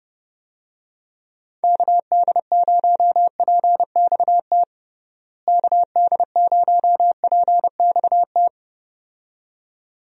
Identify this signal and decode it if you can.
Morse code, heard as a keyed tone, about 30 wpm, 700 Hz. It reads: KD0PXT KD0PXT
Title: USA Call Signs - 6 Characters - Repeated 2x - 30wpm